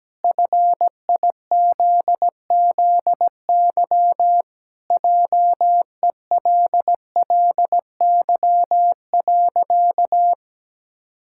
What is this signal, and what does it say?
Morse code, heard as a keyed tone, about 17 wpm, 700 Hz. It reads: FIZZY JELLY.
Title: Pangram Challenge v2 - 17wpm